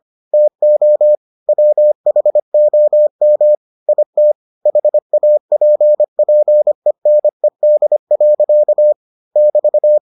Morse code, heard as a keyed tone, about 25 wpm, 600 Hz. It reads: TO WHOM IT HAPPENED. <BT>